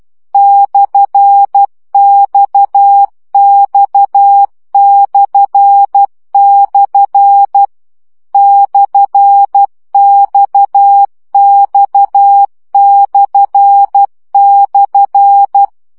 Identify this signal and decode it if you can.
Morse code, heard as a keyed tone, about 12 wpm, 800 Hz. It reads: /XX// /XX//
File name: CW gyakorlas x per 10 baud